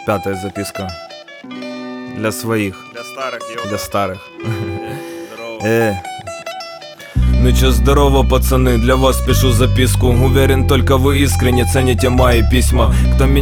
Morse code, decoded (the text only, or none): none